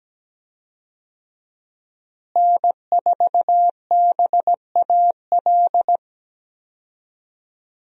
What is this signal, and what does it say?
Morse code, heard as a keyed tone, about 17 wpm, 700 Hz. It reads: N4BAL
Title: USA Call Signs - 5 Characters - Repeated 3x - 17wpm